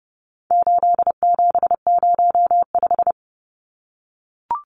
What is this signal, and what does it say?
Morse code, heard as a keyed tone, about 30 wpm, 700 Hz. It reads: 8705
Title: Numbers - 4-Digit 30wpm